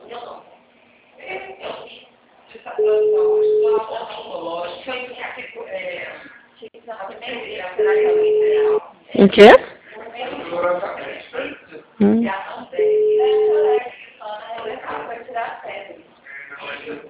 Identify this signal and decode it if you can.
Ringback tone (Europe, 425 Hz), with speech.